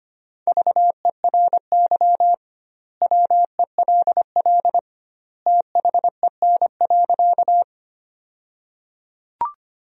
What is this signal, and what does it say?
Morse code, heard as a keyed tone, about 25 wpm, 700 Hz. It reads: VERY WELL THEN.
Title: 2x Word-Spacing - Sentences from Top 200 Words - 25wpm